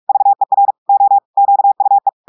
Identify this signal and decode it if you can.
Morse code, heard as a keyed tone, about 46 wpm, 800 Hz. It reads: VER O QUE
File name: audios / portugais / portugais texte en clair 2 vitesse 45 mots minutes